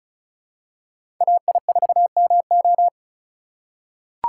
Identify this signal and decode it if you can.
Morse code, heard as a keyed tone, about 36 wpm, 700 Hz. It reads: AI4MO
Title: USA Call Signs - 3x Character Spacing - 35wpm